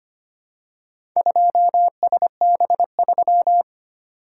Morse code, heard as a keyed tone, about 25 wpm, 700 Hz. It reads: 2SB3
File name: practice / 25wpm / Four Letter-Number Combinations - 25wpm